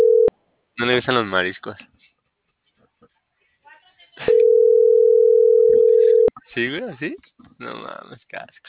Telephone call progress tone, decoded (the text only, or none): ringback tone